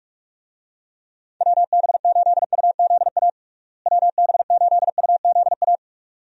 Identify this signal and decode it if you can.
Morse code, heard as a keyed tone, about 45 wpm, 700 Hz. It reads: WB8UZA WB8UZA
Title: USA Call Signs - 6 Characters - Repeated 2x - 45wpm